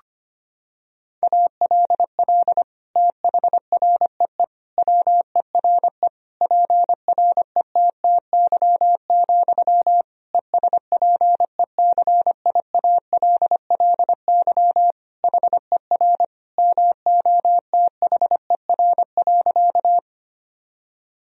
Morse code, heard as a keyed tone, about 25 wpm, 700 Hz. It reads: ALL THREE WERE PRETTY, ESPECIALLY HER MOTHER.
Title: Sentences from Top 1000 Words - Review Part 4 - 25wpm